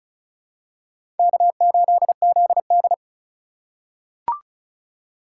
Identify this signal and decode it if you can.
Morse code, heard as a keyed tone, about 36 wpm, 700 Hz. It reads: K8ZD
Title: USA Call Signs - 4 Characters - 3x Character Spacing - 35wpm